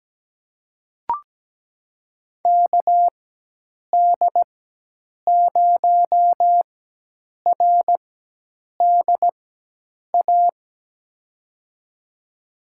Morse code, heard as a keyed tone, about 17 wpm, spaced as for 8 wpm, 700 Hz. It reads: KD0RDA
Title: USA Call Signs - 6 Characters - 4x Character Spacing - 17wpm